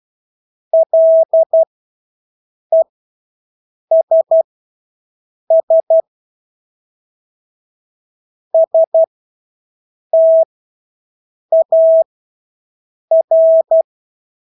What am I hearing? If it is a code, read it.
Morse code, heard as a keyed tone, about 12 wpm, spaced as for 6 wpm, 650 Hz. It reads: LESS STAR
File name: Engelske ord 12-6 wpm